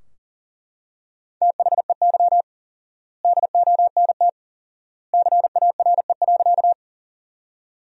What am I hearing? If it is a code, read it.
Morse code, heard as a keyed tone, about 40 wpm, 700 Hz. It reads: THEY DONT CARE.
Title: 4x Word-Spacing - Sentences from Top 300 Words - 40wpm